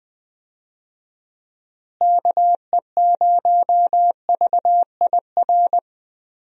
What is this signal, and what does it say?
Morse code, heard as a keyed tone, about 21 wpm, 700 Hz. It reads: KE0VIR
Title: USA Call Signs - 6 Characters - Repeated 3x - 20wpm